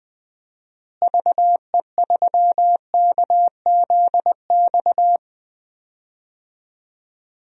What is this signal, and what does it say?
Morse code, heard as a keyed tone, about 20 wpm, 700 Hz. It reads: VE3KZX
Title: DXCC-100 Call Signs - 6 Characters - Repeated 3x - 20wpm